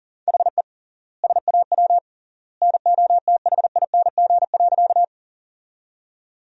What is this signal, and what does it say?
Morse code, heard as a keyed tone, about 40 wpm, 700 Hz. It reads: HE SAW NOTHING.
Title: 3x Word-Spacing - Sentences from Top 500 Words - 40wpm